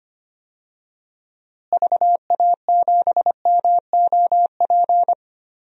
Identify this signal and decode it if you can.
Morse code, heard as a keyed tone, about 25 wpm, 700 Hz. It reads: VA7MOP
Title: DXCC-100 Call Signs - 6 Characters - Repeated 3x - 25wpm